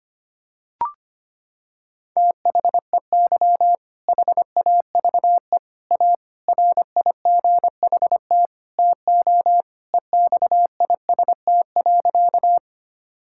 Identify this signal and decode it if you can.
Morse code, heard as a keyed tone, about 25 wpm, 700 Hz. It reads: THEY HAVE A RIGHT TO EXIST.
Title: Sentences from Top 1000 Words - Review Part 5 - 25wpm